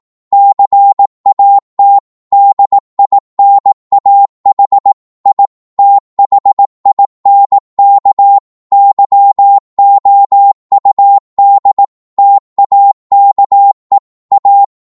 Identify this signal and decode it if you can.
Morse code, heard as a keyed tone, about 18 wpm, 800 Hz. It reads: CAT DINAH I THINK YOUD TAKE A
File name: Morse / psg-18wpm